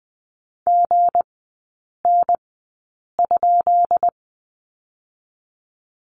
Morse code, heard as a keyed tone, about 19 wpm, spaced as for 8 wpm, 690 Hz. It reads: GN?
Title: 159 Lesson 26e - Letter g - Three Characters 20wpm